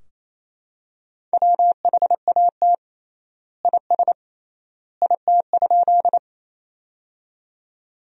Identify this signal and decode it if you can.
Morse code, heard as a keyed tone, about 28 wpm, 700 Hz. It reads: WHAT IS IT?
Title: 3x Word-Spacing - Sentences from Top 100 Words - 28wpm